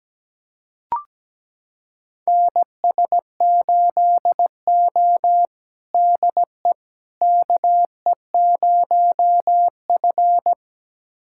Morse code, heard as a keyed tone, about 17 wpm, 700 Hz. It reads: NS8O DE KE0F
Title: USA Callsign DE Callsign - 4 Characters - 17wpm